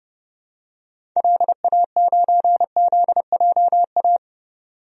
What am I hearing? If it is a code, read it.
Morse code, heard as a keyed tone, about 29 wpm, 700 Hz. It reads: LA9ZJA
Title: DXCC-100 Call Signs - 6 Characters - 30wpm